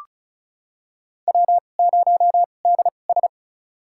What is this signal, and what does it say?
Morse code, heard as a keyed tone, about 34 wpm, spaced as for 25 wpm, 700 Hz. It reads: W0DS
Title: USA Call Signs - 4 Characters - 2x Character Spacing - 35wpm